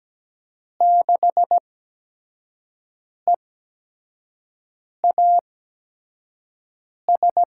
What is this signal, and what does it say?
Morse code, heard as a keyed tone, about 17 wpm, spaced as for 5 wpm, 700 Hz. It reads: 6EAS
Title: USA Call Signs - 6 Characters - 8x Character Spacing - 17wpm